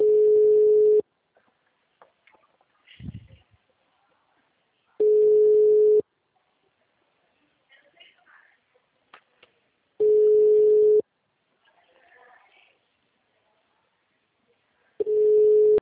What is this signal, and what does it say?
Ringback tone (Europe, 425 Hz), with speech.
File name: out- -2010- - - .38343